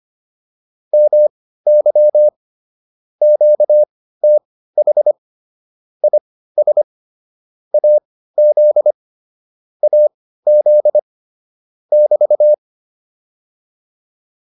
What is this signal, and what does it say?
Morse code, heard as a keyed tone, about 25 wpm, spaced as for 15 wpm, 600 Hz. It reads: MY QTH IS AZ AZ <BT>